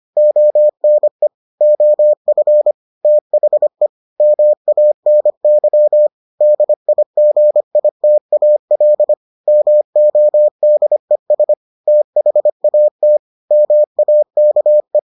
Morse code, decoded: ONE OF THE MANY DIGITAL MODES THAT MAKE

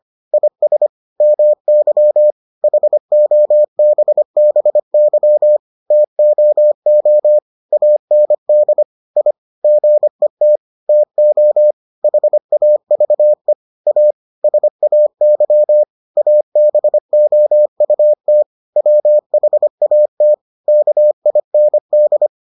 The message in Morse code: IS MY HOBBY TOO AND I GET TO HAVE A SAY ABOUT WHAT KIND

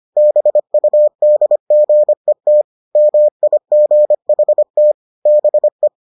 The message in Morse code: BUDGET MIGHT BE

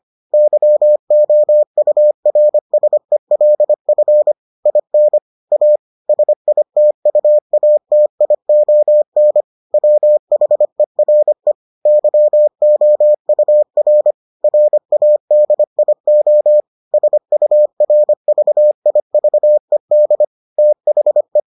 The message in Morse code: YOURSELF IN A SITUATION WHERE YOUR RADIO SURVIVED THE

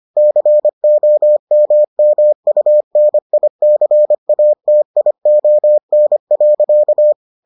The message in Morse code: COMMUNICATION.

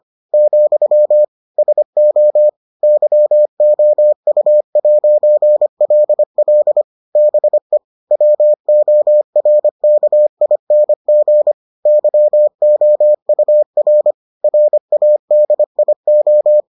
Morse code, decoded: , SO YOU'LL BE WORKING YOUR RADIO

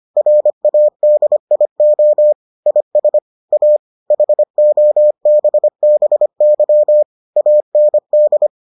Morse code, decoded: RADIO IS A HOBBY AND

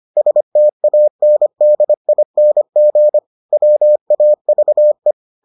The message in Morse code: STANDING WAVE

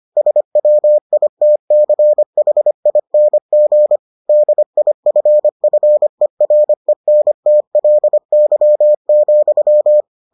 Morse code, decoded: SWITCHING DIFFERENTLY,

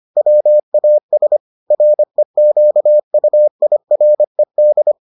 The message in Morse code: WAS REQUIRED

